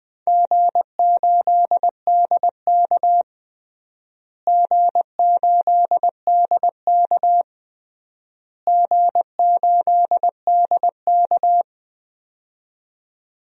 Morse code, heard as a keyed tone, about 20 wpm, 700 Hz. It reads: G8DK G8DK G8DK
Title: DXCC-100 Call Signs - 4 Characters - Repeated 3x - 20wpm